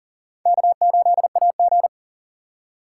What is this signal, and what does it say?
Morse code, heard as a keyed tone, about 51 wpm, 700 Hz. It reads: K8AG